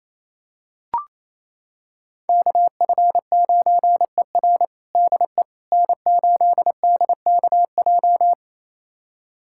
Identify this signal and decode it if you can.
Morse code, heard as a keyed tone, about 28 wpm, 700 Hz. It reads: KF9ER DE N8DKJ
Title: USA Callsign DE Callsign 28wpm